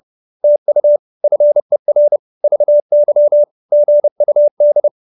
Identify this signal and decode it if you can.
Morse code, heard as a keyed tone, about 30 wpm, 590 Hz. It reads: TU FER VY GUD
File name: QSO309 30